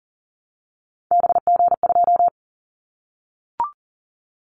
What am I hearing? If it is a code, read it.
Morse code, heard as a keyed tone, about 40 wpm, 700 Hz. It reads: BG2
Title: Callsign Prefixes for Top 100 DXCC - 40wpm